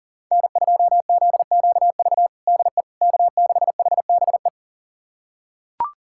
Morse code, heard as a keyed tone, about 40 wpm, 700 Hz. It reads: N2ZQV DE K6HBE
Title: USA Callsign DE Callsign 40wpm